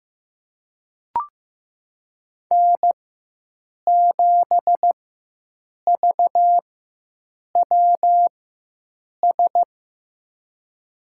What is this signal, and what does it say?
Morse code, heard as a keyed tone, about 15 wpm, spaced as for 7 wpm, 700 Hz. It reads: N7VWS
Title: USA Call Signs - 4x Character Spacing - 15wpm